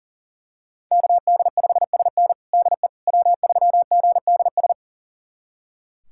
Morse code, heard as a keyed tone, about 40 wpm, 700 Hz. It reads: KD5SN DE W3GDS
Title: USA Callsign DE Callsign 40wpm